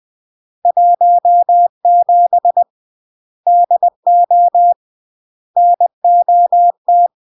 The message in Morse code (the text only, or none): 17 DO NOT